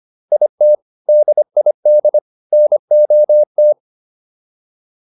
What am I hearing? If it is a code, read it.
Morse code, heard as a keyed tone, about 25 wpm, 600 Hz. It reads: IT DID NOT